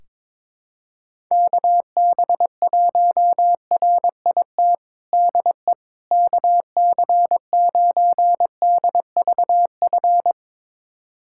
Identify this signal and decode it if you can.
Morse code, heard as a keyed tone, about 22 wpm, 700 Hz. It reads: KB1RIT DE KC9DVF
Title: USA Callsign DE Callsign - 6 Characters - 22wpm